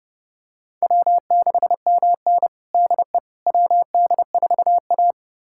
Morse code, heard as a keyed tone, about 32 wpm, 700 Hz. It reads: W6MN DE WD4A
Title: USA Callsign DE Callsign - 4 Characters - 30wpm